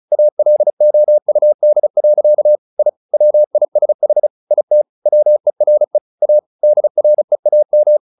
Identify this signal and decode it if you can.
Morse code, heard as a keyed tone, about 34 wpm, 600 Hz. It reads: ALOUD. I WISH IT WERE A DREAM
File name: the old man and the sea 2 35wpm